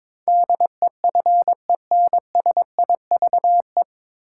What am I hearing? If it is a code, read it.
Morse code, heard as a keyed tone, about 22 wpm, 700 Hz. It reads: DEFENSIVE
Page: Morse Code Ninja practice 22wpm